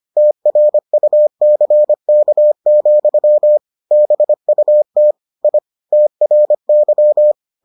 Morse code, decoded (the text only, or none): TRUCK, BUT I TRY